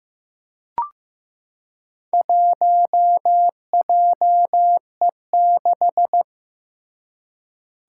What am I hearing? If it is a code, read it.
Morse code, heard as a keyed tone, about 15 wpm, 700 Hz. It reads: 1JE6
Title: Four Challenging Letter-Number Combinations 15wpm